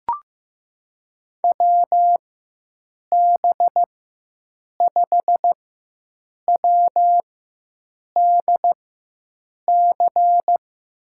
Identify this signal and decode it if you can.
Morse code, heard as a keyed tone, about 15 wpm, spaced as for 7 wpm, 700 Hz. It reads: WB5WDC
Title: USA Call Signs - 6 Characters - 4x Character Spacing - 15wpm